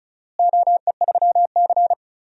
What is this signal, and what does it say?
Morse code, heard as a keyed tone, about 36 wpm, 700 Hz. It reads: OE3C
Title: DXCC-100 Call Signs - 4 Characters - Repeated 3x - 35wpm